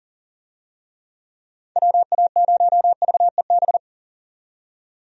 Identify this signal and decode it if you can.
Morse code, heard as a keyed tone, about 40 wpm, 700 Hz. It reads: WA0VEB
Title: USA Call Signs - 6 Characters - 40wpm